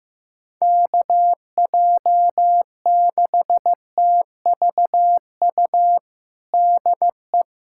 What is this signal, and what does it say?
Morse code, heard as a keyed tone, about 15 wpm, 700 Hz. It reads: KJ6TVU DE
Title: USA Callsign DE Callsign 15wpm